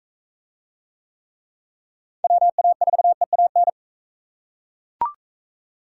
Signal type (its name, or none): Morse code